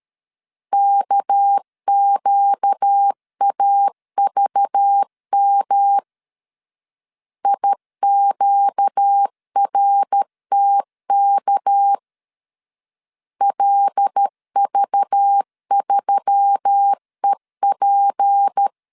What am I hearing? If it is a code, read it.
Morse code, heard as a keyed tone, about 13 wpm, 800 Hz. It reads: KQAVM IQRTK LV3EP
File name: LEZIONI / 6 PKX0 / 6 PK 12wpm 2006